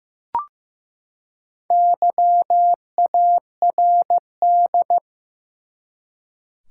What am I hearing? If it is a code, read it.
Morse code, heard as a keyed tone, about 15 wpm, 700 Hz. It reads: YARD
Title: Four Letter Words - 15wpm